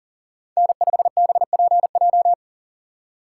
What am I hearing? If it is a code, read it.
Morse code, heard as a keyed tone, about 40 wpm, 700 Hz. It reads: N5BPJ
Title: USA Call Signs - 5 Characters - 4x Character Spacing - 40wpm